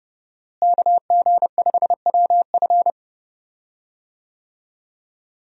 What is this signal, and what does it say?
Morse code, heard as a keyed tone, about 30 wpm, 700 Hz. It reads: KG5WF
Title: USA Call Signs - 5 Characters - Repeated 3x - 30wpm